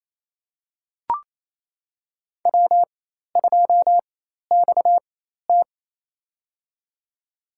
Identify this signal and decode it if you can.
Morse code, heard as a keyed tone, about 28 wpm, spaced as for 13 wpm, 700 Hz. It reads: W2XT
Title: USA Call Signs - 4x Character Spacing - 28wpm